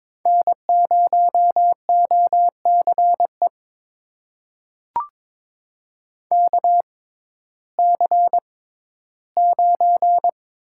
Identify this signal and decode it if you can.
Morse code, heard as a keyed tone, about 22 wpm, spaced as for 14 wpm, 700 Hz. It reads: N0OCE KC9
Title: USA Call Signs - 5 Characters - 6x Character Spacing - 22wpm